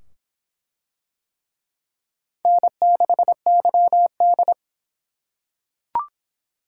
Morse code, heard as a keyed tone, about 23 wpm, 700 Hz. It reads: N6YD